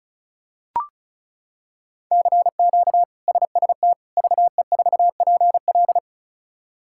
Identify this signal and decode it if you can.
Morse code, heard as a keyed tone, about 36 wpm, 700 Hz. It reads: CQ SST VE4PL
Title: K1USN SST QSO Practice - 35wpm